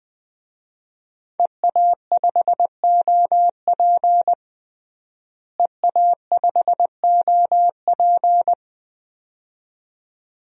Morse code, heard as a keyed tone, about 20 wpm, 700 Hz. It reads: EA5OP EA5OP
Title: DXCC-100 Call Signs - 5 Characters - Repeated 2x - 20wpm